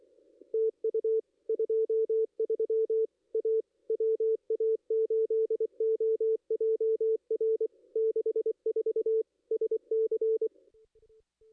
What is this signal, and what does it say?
Morse code, heard as a keyed tone, about 22 wpm, 440 Hz. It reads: TU 23 A WA8OJR 64 SC
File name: contests / ss-cw-k5tr-op-n5rz -2200